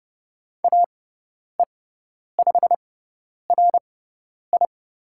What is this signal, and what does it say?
Morse code, heard as a keyed tone, about 30 wpm, spaced as for 10 wpm, 700 Hz. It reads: AE5RI
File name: Lesson 16g - Letter c - Call Signs 30wpm